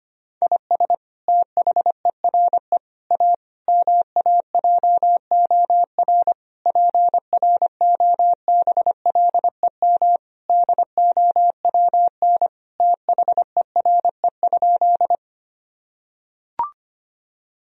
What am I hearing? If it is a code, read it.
Morse code, heard as a keyed tone, about 25 wpm, 700 Hz. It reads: IS THERE A MAJOR PROBLEM DOWN THERE?
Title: Sentences from Top 1000 Words - Review Part 4 - 25wpm